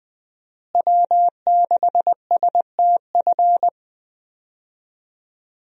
Morse code, heard as a keyed tone, about 21 wpm, 700 Hz. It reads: W6STF
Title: USA Call Signs - 5 Characters - Repeated 2x - 20wpm